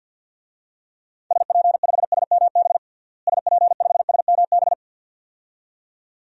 Morse code, heard as a keyed tone, about 50 wpm, 700 Hz. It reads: SP5SMB SP5SMB
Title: DXCC-100 Call Signs - 6 Characters - Repeated 2x - 50wpm